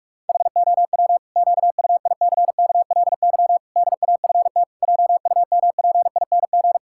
Morse code, heard as a keyed tone, about 46 wpm, 700 Hz. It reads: HOW QUICKLY DAFT JUMPING
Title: Pangram Challenge v2 - 45wpm